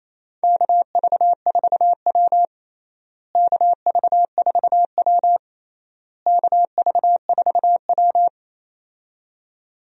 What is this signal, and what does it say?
Morse code, heard as a keyed tone, about 28 wpm, 700 Hz. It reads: KV4W KV4W KV4W
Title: USA Call Signs - 4 Characters - Repeated 3x - 28wpm